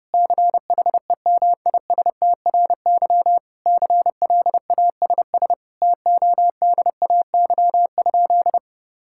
Morse code, heard as a keyed tone, about 30 wpm, 700 Hz. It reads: CHEMISTRY CLASS TODAY?
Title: Sentences from Top 1000 Words - Review Part 6 - 30wpm